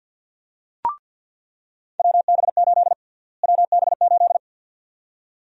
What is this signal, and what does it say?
Morse code, heard as a keyed tone, about 49 wpm, 700 Hz. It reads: WB8 WB8
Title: USA Call Sign Prefixes - Repeated 2x - 50wpm